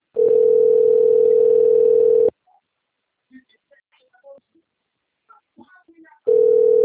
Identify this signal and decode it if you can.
Ringback tone (North America), 440 + 480 Hz.